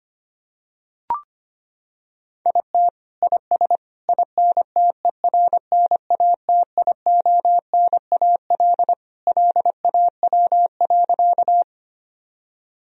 Morse code, heard as a keyed tone, about 25 wpm, 700 Hz. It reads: IT IS INTERNATIONAL LAW.
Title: Sentences from Top 1000 Words - Review Part 4 - 25wpm